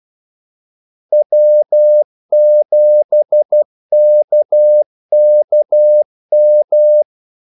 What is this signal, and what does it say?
Morse code, heard as a keyed tone, about 12 wpm, 600 Hz. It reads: W7KKM
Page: 2 ekstra ordmellemrum mellem kaldesignalerne